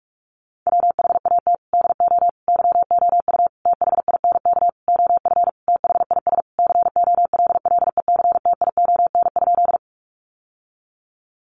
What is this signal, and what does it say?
Morse code, heard as a keyed tone, about 44 wpm, 700 Hz. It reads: WHAT DO YOU THINK OF THIS COLLECTION?
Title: Sentences from Top 1200 Words 45wpm